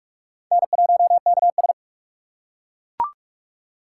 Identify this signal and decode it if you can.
Morse code, heard as a keyed tone, about 42 wpm, 700 Hz. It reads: N1KS